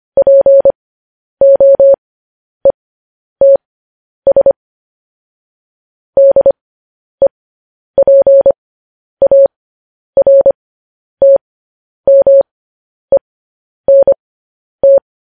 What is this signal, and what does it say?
Morse code, heard as a keyed tone, about 25 wpm, spaced as for 10 wpm, 550 Hz. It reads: POETS DEPARTMENT